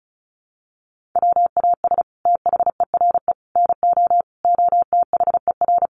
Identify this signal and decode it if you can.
Morse code, heard as a keyed tone, about 34 wpm, 700 Hz. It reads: WAS THERE NO OTHER
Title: Sentences from Top 1400 Words 35wpm